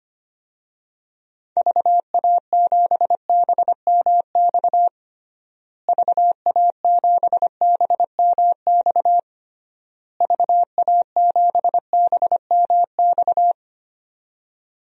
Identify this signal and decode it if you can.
Morse code, heard as a keyed tone, about 25 wpm, 700 Hz. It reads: VA7BMX VA7BMX VA7BMX
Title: DXCC-100 Call Signs - 6 Characters - Repeated 3x - 25wpm